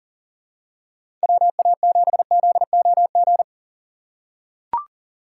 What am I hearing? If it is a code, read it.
Morse code, heard as a keyed tone, about 40 wpm, 700 Hz. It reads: WA7ZOG
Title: USA Call Signs - 6 Characters - 4x Character Spacing - 40wpm